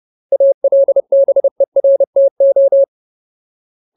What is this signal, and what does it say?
Morse code, heard as a keyed tone, about 30 wpm, 550 Hz. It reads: ALBERTO
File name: Ultimate Word List / 30WPM Ultimate